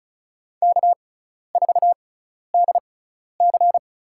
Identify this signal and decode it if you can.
Morse code, heard as a keyed tone, about 34 wpm, spaced as for 12 wpm, 700 Hz. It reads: K4DC
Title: USA Call Signs - 4 Characters - 6x Character Spacing - 35wpm